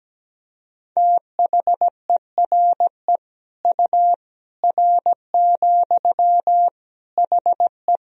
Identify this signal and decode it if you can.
Morse code, heard as a keyed tone, about 17 wpm, 700 Hz. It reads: THERE U R, HE
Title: Sentences from Top 100 Words and Abbreviations 17wpm